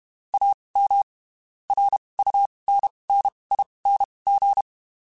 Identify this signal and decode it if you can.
Morse code, heard as a keyed tone, about 32 wpm, spaced as for 23 wpm, 800 Hz. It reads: AM RUNNING
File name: (spaced) / QSOs Spaced-mp3s / WA7QOX 2 @30 sp